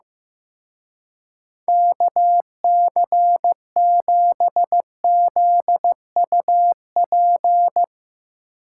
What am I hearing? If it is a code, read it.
Morse code, heard as a keyed tone, about 15 wpm, 700 Hz. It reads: KC7ZUP
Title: Speed Racing - USA 2x3 Call Signs 15wpm